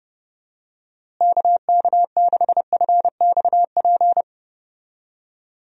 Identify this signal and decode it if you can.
Morse code, heard as a keyed tone, about 30 wpm, 700 Hz. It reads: KK6FXP
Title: USA Call Signs - 6 Characters - Repeated 2x - 30wpm